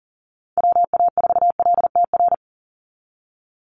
Morse code, heard as a keyed tone, about 40 wpm, 700 Hz. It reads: WA4LTR
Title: USA Call Signs 40wpm